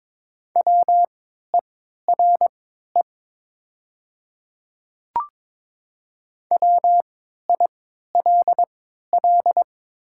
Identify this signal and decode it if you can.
Morse code, heard as a keyed tone, about 22 wpm, spaced as for 12 wpm, 700 Hz. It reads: WERE WILL
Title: 3x Character Spacing - Top 100 Words - 22wpm